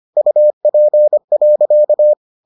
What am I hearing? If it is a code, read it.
Morse code, heard as a keyed tone, about 25 wpm, 600 Hz. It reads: UP.